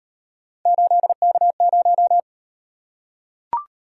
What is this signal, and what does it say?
Morse code, heard as a keyed tone, about 39 wpm, 700 Hz. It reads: 8K0